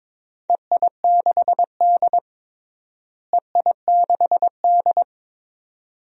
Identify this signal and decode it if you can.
Morse code, heard as a keyed tone, about 21 wpm, 700 Hz. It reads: EI6D EI6D
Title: DXCC-100 Call Signs - 4 Characters - Repeated 2x - 22wpm